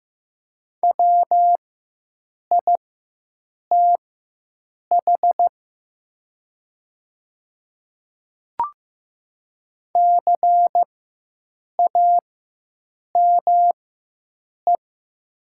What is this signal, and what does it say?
Morse code, heard as a keyed tone, about 15 wpm, spaced as for 7 wpm, 700 Hz. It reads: WITH CAME
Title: 4x Character Spacing - Top 200 Words - 15wpm